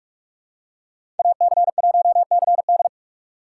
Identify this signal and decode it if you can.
Morse code, heard as a keyed tone, about 47 wpm, 690 Hz. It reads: AC1CD
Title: USA Call Signs - 5 Characters - 6x Character Spacing - 45wpm